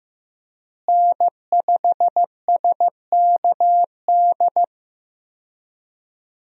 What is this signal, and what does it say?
Morse code, heard as a keyed tone, about 15 wpm, 700 Hz. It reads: N5SKD
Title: USA Call Signs - 5 Characters - Repeated 2x - 15wpm